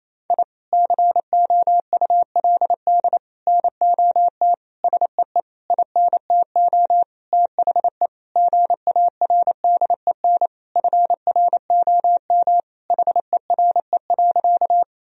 Morse code, heard as a keyed tone, about 28 wpm, 700 Hz. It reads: I COULD NOT SEE INTO THE GARDEN FROM HERE.
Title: Sentences from Top 1000 Words - Review Part 4 - 28wpm